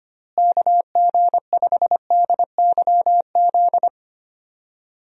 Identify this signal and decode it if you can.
Morse code, heard as a keyed tone, about 25 wpm, 700 Hz. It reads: KG5DYZ
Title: USA Call Signs - 6 Characters - Repeated 2x - 25wpm